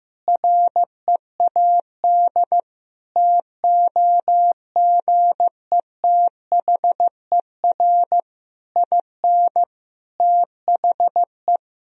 Morse code, heard as a keyed tone, about 16 wpm, 700 Hz. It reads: READ TOGETHER IN THE
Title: Sentences from Top 1000 Words - Review Part 3 - 15wpm